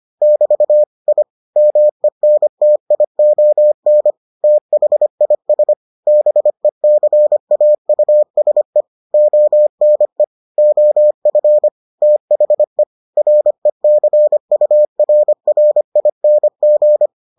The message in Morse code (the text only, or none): <BT> I MENTION THIS BECAUSE ONE OF THE RECURRING